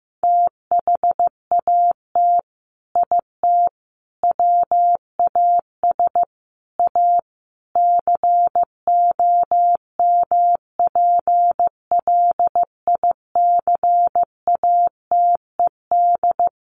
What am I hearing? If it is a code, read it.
Morse code, heard as a keyed tone, about 15 wpm, 700 Hz. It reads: THAT IT WAS A COMPLICATED
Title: Sentences from Top 1200 Words 15wpm